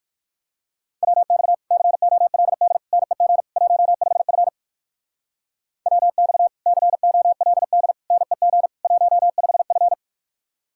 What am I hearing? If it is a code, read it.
Morse code, heard as a keyed tone, about 48 wpm, 700 Hz. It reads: WX COLD NEG 15F WX COLD NEG 15F
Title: Speed-Racing Common QSO Phrases Encore - 45wpm